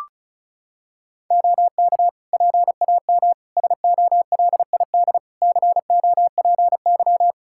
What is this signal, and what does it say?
Morse code, heard as a keyed tone, about 36 wpm, 700 Hz. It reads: OK PAM SOLID COPY
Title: Practice QSO - Exchange 3 - More Info - 35wpm